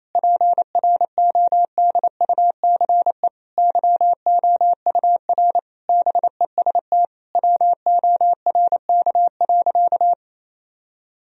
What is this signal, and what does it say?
Morse code, heard as a keyed tone, about 28 wpm, 700 Hz. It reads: PRODUCE YOUR BEST WORK.
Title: Sentences from Top 1000 Words - Review Part 3 - 28wpm